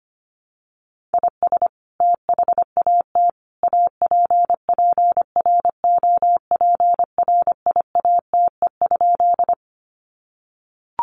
Morse code, heard as a keyed tone, about 25 wpm, 700 Hz. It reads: IS THAT APPROPRIATE?
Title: Sentences from Top 900 Words 25wpm